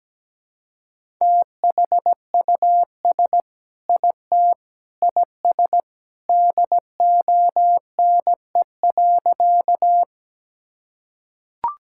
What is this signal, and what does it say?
Morse code, heard as a keyed tone, about 17 wpm, 700 Hz. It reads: THUS IT IS DONE.
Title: Sentences from Top 1000 Words - Review Part 5 - 17wpm